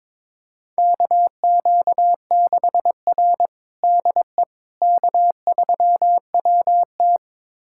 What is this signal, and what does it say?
Morse code, heard as a keyed tone, about 22 wpm, 700 Hz. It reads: KQ6R DE K3WT
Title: USA Callsign DE Callsign - 4 Characters - 22wpm